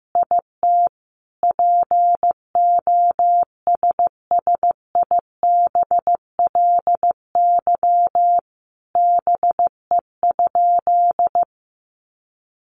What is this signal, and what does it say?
Morse code, heard as a keyed tone, about 15 wpm, 700 Hz. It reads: IT POSSIBLY BE?
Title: Sentences from Top 1300 Words 15wpm